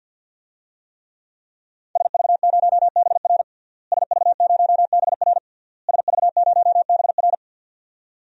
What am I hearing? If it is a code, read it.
Morse code, heard as a keyed tone, about 52 wpm, 700 Hz. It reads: SV0BR SV0BR SV0BR
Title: DXCC-100 Call Signs - 5 Characters - Repeated 3x - 50wpm